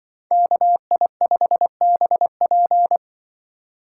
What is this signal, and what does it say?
Morse code, heard as a keyed tone, about 25 wpm, 700 Hz. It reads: KI5BP
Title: Speed Racing - USA 2x2 Call Signs 20wpm